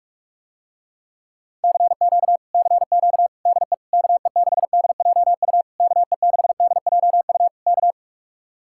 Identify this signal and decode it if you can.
Morse code, heard as a keyed tone, about 49 wpm, 700 Hz. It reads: CQ CQ DE KE6DJU KE6DJU K